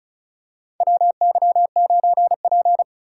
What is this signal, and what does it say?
Morse code, heard as a keyed tone, about 34 wpm, 700 Hz. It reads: WY9P
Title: USA Call Signs - 4 Characters - 2x Character Spacing - 35wpm